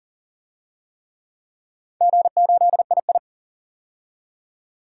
Morse code, heard as a keyed tone, about 39 wpm, 700 Hz. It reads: G8II
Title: DXCC-100 Call Signs - 4 Characters - 40wpm